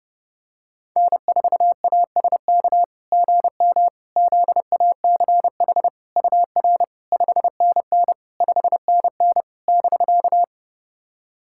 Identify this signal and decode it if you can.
Morse code, heard as a keyed tone, about 32 wpm, 660 Hz. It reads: N4ASK GM ZACH UR 5NN 5NN <BK>
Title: SOTA QSO Practice 30wpm